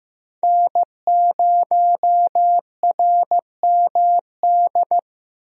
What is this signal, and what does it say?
Morse code, heard as a keyed tone, about 15 wpm, 700 Hz. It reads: N0RMD
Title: USA Call Signs - 5 Characters - Repeated 3x - 15wpm